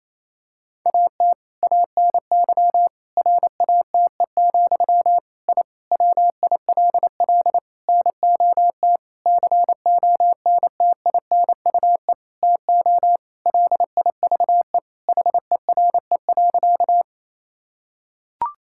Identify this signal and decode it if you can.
Morse code, heard as a keyed tone, about 27 wpm, 700 Hz. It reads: AT ANY RATE, I WILL NOT CONTINUE TO LIVE HERE.
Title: Sentences from Top 1000 Words - Review Part 6 - 28wpm